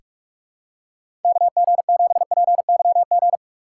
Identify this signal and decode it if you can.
Morse code, heard as a keyed tone, about 44 wpm, 700 Hz. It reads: KG7PYG
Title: USA Call Signs - 6 Characters - 6x Character Spacing - 45wpm